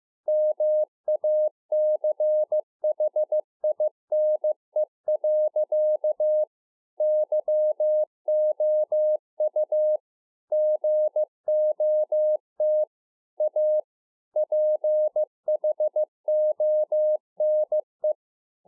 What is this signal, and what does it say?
Morse code, heard as a keyed tone, about 15 wpm, 610 Hz. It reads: MACHINE. YOU GOT A PHONE